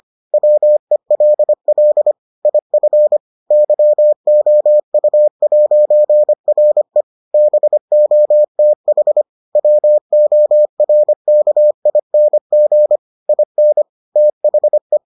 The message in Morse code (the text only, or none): WELL IF YOU'RE BOTH WORKING IN THE